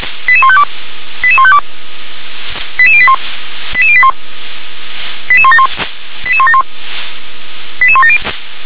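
Five-tone selective call (ZVEI), heard as quick runs of five tones, tones of about 70 ms. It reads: 88152 88152 88081 88081 88171 88171 88170
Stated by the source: ZVEI five-tone call 88220 received by SDR.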